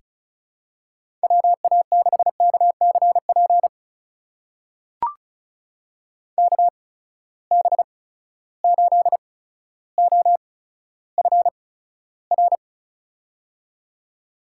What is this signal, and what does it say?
Morse code, heard as a keyed tone, about 36 wpm, spaced as for 15 wpm, 700 Hz. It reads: WA6KCP KB8OFR
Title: USA Call Signs - 6 Characters - 8x Character Spacing - 35wpm